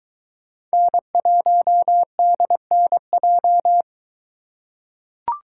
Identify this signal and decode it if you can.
Morse code, heard as a keyed tone, about 23 wpm, 700 Hz. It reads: N1DNJ